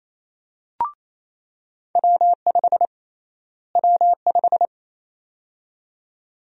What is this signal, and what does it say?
Morse code, heard as a keyed tone, about 28 wpm, 700 Hz. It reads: W5 W5
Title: USA Call Sign Prefixes - Repeated 2x - 28wpm